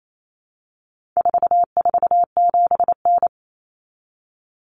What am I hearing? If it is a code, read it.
Morse code, heard as a keyed tone, about 28 wpm, 700 Hz. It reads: 447N
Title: Common Cut Numbers - 4-Digit 28wpm